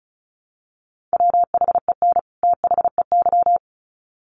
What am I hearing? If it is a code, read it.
Morse code, heard as a keyed tone, about 34 wpm, 700 Hz. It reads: WHEN THEY
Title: Sets of 2 Words 35wpm